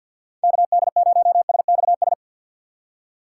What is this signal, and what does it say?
Morse code, heard as a keyed tone, about 53 wpm, 700 Hz. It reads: KD0SXS
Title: USA Call Signs - 6 Characters - 3x Character Spacing - 50wpm